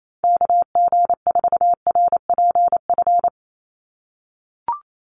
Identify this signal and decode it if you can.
Morse code, heard as a keyed tone, about 28 wpm, 700 Hz. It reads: KG4RPF
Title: USA Call Signs 28wpm